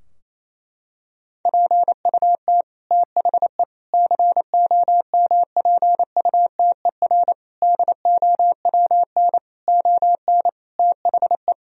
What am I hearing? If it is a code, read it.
Morse code, heard as a keyed tone, about 28 wpm, 700 Hz. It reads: PUT THE COMPUTER DOWN ON THE
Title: Sentences from Top 1000 Words - Review Part 3 - 28wpm